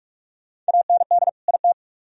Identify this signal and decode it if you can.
Morse code, heard as a keyed tone, about 60 wpm, spaced as for 53 wpm, 690 Hz. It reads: AND IT
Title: Speed-Racing - Sets of 2 Words 45wpm